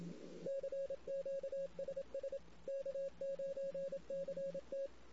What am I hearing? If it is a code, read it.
Morse code, heard as a keyed tone, about 27 wpm, 550 Hz. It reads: CQ SS K9CT
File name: CW / 2008-ss-cw-k5tr-op-n5rz -16:00